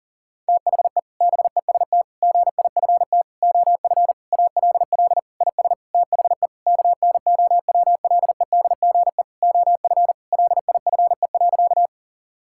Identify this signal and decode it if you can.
Morse code, heard as a keyed tone, about 40 wpm, 700 Hz. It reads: THE BEST GIFT OF ALL IS THE KNOWLEDGE OF LIFE.
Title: Sentences from Top 1000 Words - Review Part 6 - 40wpm